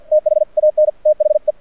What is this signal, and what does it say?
Morse code, heard as a keyed tone, about 48 wpm, 610 Hz. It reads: THAN THE
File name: archived / CRSnet